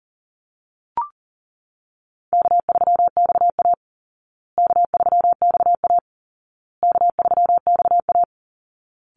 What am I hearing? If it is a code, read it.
Morse code, heard as a keyed tone, about 40 wpm, 700 Hz. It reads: K3XA K3XA K3XA
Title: Call Signs Repeated 2x - 40wpm